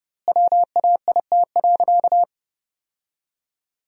Morse code, heard as a keyed tone, about 32 wpm, 700 Hz. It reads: WAIT.
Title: 2x Word-Spacing - Sentences from Top 500 Words - 30wpm